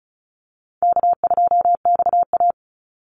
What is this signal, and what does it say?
Morse code, heard as a keyed tone, about 34 wpm, 700 Hz. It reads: K2XA
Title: Easy Call Signs Repeated 2x - 35wpm